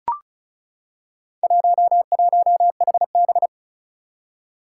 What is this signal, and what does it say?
Morse code, heard as a keyed tone, about 36 wpm, 700 Hz. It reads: 11HB
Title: Four Challenging Letter-Number Combinations 35wpm